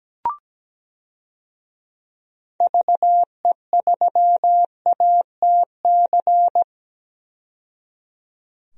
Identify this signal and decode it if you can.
Morse code, heard as a keyed tone, about 17 wpm, 700 Hz. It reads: VE3ATC
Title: DXCC-100 Call Signs - 6 Characters - 17wpm